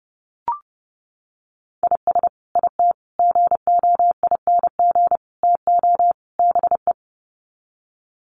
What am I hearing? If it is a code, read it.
Morse code, heard as a keyed tone, about 30 wpm, 700 Hz. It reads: IS IT GOING TO BE
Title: Sets of 5 Words Encore2 30wpm